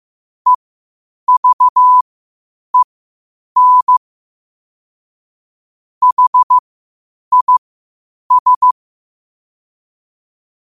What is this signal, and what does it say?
Morse code, heard as a keyed tone, about 15 wpm, spaced as for 8 wpm, 1000 Hz. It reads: EVEN HIS
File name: qotdfiles / 8wpm